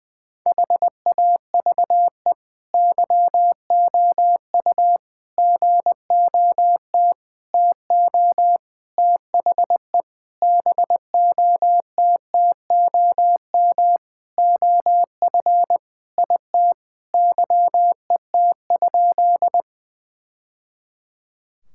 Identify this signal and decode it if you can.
Morse code, heard as a keyed tone, about 21 wpm, 700 Hz. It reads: HAVE YOU GOT TO THE BOTTOM OF IT YET?
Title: Sentences from Top 1000 Words - Review Part 6 - 20wpm